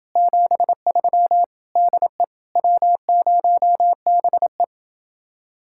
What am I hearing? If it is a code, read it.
Morse code, heard as a keyed tone, about 29 wpm, 700 Hz. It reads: 73 DE W0BE